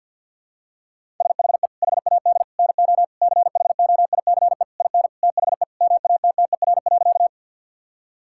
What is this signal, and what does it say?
Morse code, heard as a keyed tone, about 50 wpm, 700 Hz. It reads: SHE HAD NO CHOICE IN THE MATTER.
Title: Sentences from Top 1000 Words - Review Part 6 - 50wpm